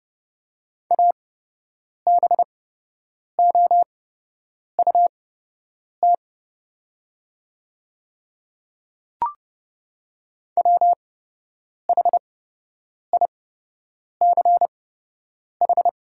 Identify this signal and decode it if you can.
Morse code, heard as a keyed tone, about 30 wpm, spaced as for 8 wpm, 700 Hz. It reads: ABOUT WHICH
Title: 8x Character Spacing - Top 100 Words - 30wpm